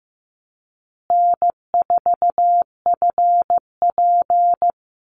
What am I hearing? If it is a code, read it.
Morse code, heard as a keyed tone, about 15 wpm, 700 Hz. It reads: N4FP
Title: Easy Call Signs Repeated 2x - 15wpm